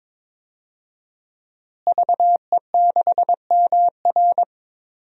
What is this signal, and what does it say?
Morse code, heard as a keyed tone, about 22 wpm, 700 Hz. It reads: VE6MR
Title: DXCC-100 Call Signs - 5 Characters - Repeated 3x - 22wpm